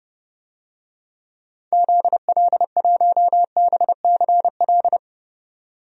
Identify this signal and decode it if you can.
Morse code, heard as a keyed tone, about 30 wpm, 700 Hz. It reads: ZL1BCL
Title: DXCC-100 Call Signs - 6 Characters - 30wpm